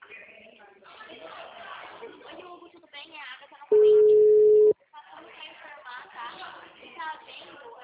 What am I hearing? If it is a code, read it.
Ringback tone (Europe, 425 Hz), with speech.